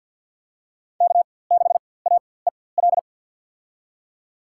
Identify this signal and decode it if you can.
Morse code, heard as a keyed tone, about 48 wpm, spaced as for 23 wpm, 700 Hz. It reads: K6AEL
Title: USA Call Signs - 5 Characters - 4x Character Spacing - 50wpm